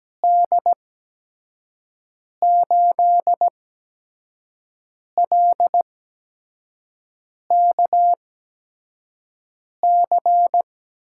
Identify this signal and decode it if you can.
Morse code, heard as a keyed tone, about 17 wpm, spaced as for 5 wpm, 700 Hz. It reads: D8LKC
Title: USA Call Signs - 6 Characters - 8x Character Spacing - 17wpm